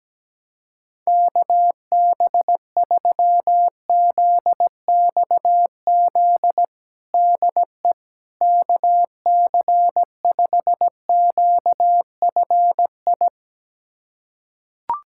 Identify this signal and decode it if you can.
Morse code, heard as a keyed tone, about 17 wpm, 700 Hz. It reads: KB3ZXZ DE KC5QFI
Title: USA Callsign DE Callsign - 6 Characters - 17wpm